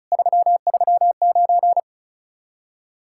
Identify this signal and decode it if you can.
Morse code, heard as a keyed tone, about 44 wpm, 700 Hz. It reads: 339